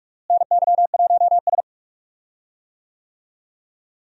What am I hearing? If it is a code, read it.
Morse code, heard as a keyed tone, about 48 wpm, 700 Hz. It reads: NY1S
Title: USA Call Signs - 4 Characters - Repeated 3x - 45wpm